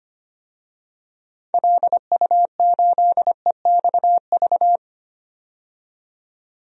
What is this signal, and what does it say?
Morse code, heard as a keyed tone, about 25 wpm, 700 Hz. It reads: LU8EXV
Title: DXCC-100 Call Signs - 6 Characters - 25wpm